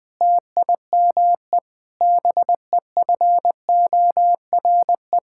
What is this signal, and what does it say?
Morse code, heard as a keyed tone, about 20 wpm, 700 Hz. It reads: TIME BEFORE
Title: Sentences with 8-Word Phrases - 20wpm